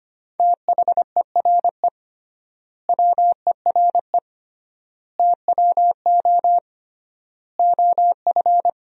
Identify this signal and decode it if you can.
Morse code, heard as a keyed tone, about 25 wpm, 700 Hz. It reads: THERE WERE TWO OF
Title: 3x Word-Spacing - Sentences from Top 100 Words - 25wpm